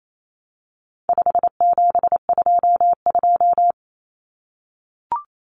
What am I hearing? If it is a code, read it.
Morse code, heard as a keyed tone, about 28 wpm, 700 Hz. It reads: 5722
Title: Common Cut Numbers - 4-Digit 28wpm